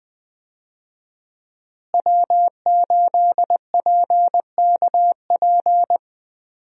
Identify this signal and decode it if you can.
Morse code, heard as a keyed tone, about 20 wpm, 700 Hz. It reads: W8PKP
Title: USA Call Signs - 5 Characters - Repeated 2x - 20wpm